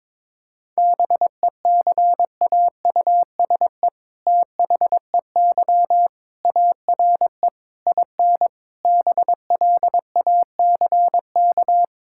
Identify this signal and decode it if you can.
Morse code, heard as a keyed tone, about 22 wpm, 690 Hz. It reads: BECAUSE THEY ARE IN BLACK
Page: morse-code-ninja-practice-22wpm